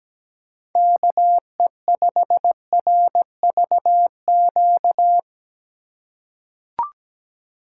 Morse code, heard as a keyed tone, about 17 wpm, 700 Hz. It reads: KE5RVQ
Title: USA Call Signs - 6 Characters - 2x Character Spacing - 17wpm